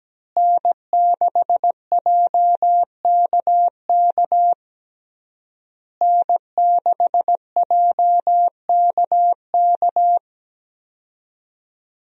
Morse code, heard as a keyed tone, about 17 wpm, 700 Hz. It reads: N6JKK N6JKK
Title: USA Call Signs - 5 Characters - Repeated 3x - 17wpm